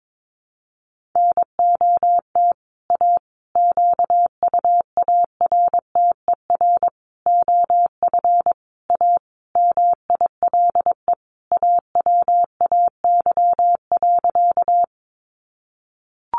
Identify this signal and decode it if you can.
Morse code, heard as a keyed tone, about 22 wpm, 700 Hz. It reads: NOT A QUARTER OF A MILE AWAY.
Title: Sentences from Top 1300 Words 22wpm